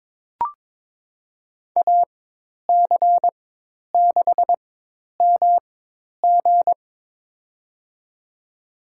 Morse code, heard as a keyed tone, about 22 wpm, spaced as for 10 wpm, 700 Hz. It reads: AC6MG
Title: USA Call Signs - 5 Characters - 4x Character Spacing - 22wpm